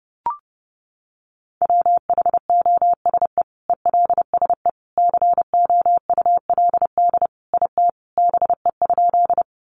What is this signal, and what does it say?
Morse code, heard as a keyed tone, about 30 wpm, 700 Hz. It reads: WHOSE ELSE COULD IT BE?
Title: Sentences from Top 1200 Words 30wpm